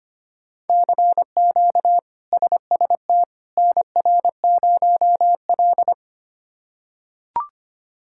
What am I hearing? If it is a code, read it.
Morse code, heard as a keyed tone, about 25 wpm, 700 Hz. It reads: CQ SST NR0L
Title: K1USN SST QSO Practice - 25wpm